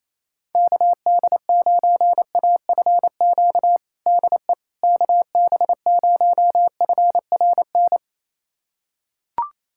Morse code, heard as a keyed tone, about 28 wpm, 700 Hz. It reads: KD9AFQ DE KB0FRN
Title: USA Callsign DE Callsign - 6 Characters - 28wpm